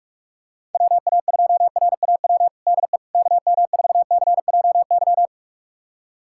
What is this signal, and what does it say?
Morse code, heard as a keyed tone, about 43 wpm, 700 Hz. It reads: WA2RAW DE KM4CJY
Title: USA Callsign DE Callsign - 6 Characters - 45wpm